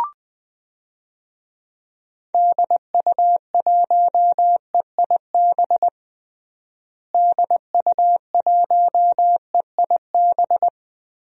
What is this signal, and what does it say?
Morse code, heard as a keyed tone, about 20 wpm, 700 Hz. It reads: DU1EIB DU1EIB
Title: DXCC-100 Call Signs - 6 Characters - Repeated 3x - 20wpm